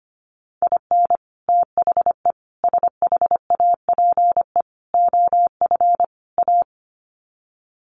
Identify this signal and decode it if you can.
Morse code, heard as a keyed tone, about 25 wpm, 700 Hz. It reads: IN THE SHAPE OF A
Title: Sets of 5 Words Encore 25wpm